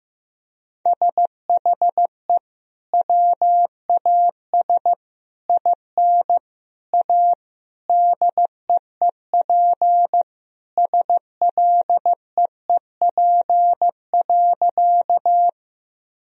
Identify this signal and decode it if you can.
Morse code, heard as a keyed tone, about 15 wpm, 700 Hz. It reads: SHE WAS IN A DEEP SLEEP.
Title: Sentences from Top 1000 Words - Review Part 3 - 15wpm